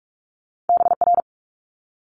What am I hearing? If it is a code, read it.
Morse code, heard as a keyed tone, about 46 wpm, 700 Hz. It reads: BR
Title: Multi-letter Phonemes Quick Fire 45wpm